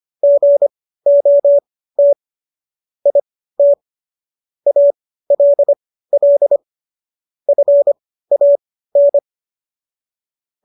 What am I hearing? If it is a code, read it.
Morse code, heard as a keyed tone, about 25 wpm, spaced as for 15 wpm, 570 Hz. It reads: GOT IT ALL FAN
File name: qso203 15